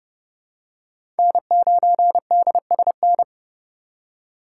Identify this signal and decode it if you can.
Morse code, heard as a keyed tone, about 30 wpm, 700 Hz. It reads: N9DSN
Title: USA Call Signs - 5 Characters - 3x Character Spacing - 30wpm